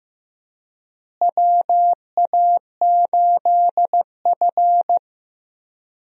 Morse code, heard as a keyed tone, about 15 wpm, 700 Hz. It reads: WA8F
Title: USA Call Signs - 4 Characters - Repeated 3x - 15wpm